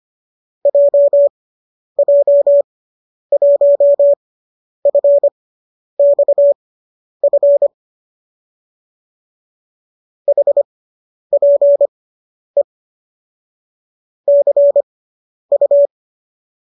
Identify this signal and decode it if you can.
Morse code, heard as a keyed tone, about 25 wpm, spaced as for 10 wpm, 570 Hz. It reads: JJ1FXF HPE CU